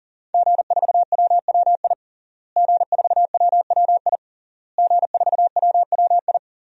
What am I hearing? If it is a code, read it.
Morse code, heard as a keyed tone, about 37 wpm, 700 Hz. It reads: G4WWI G4WWI G4WWI
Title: DXCC-100 Call Signs - 5 Characters - Repeated 3x - 40wpm